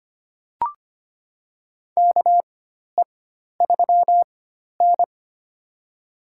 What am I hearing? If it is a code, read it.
Morse code, heard as a keyed tone, about 25 wpm, spaced as for 12 wpm, 700 Hz. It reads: KE3N
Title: USA Call Signs - 4 Characters - 4x Character Spacing - 25wpm